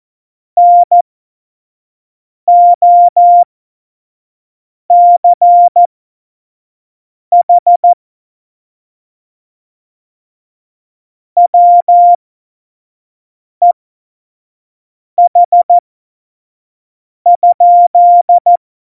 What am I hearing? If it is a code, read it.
Morse code, heard as a keyed tone, about 14 wpm, spaced as for 5 wpm, 700 Hz. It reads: NOCH WEH?
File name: Morsekurs / Witze / Witz 19. Dezember 2017, 25 F70